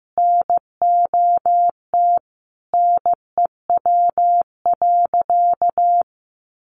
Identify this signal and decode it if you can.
Morse code, heard as a keyed tone, about 15 wpm, 700 Hz. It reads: NOT NEW.
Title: Sentences from Top 1100 Words 15wpm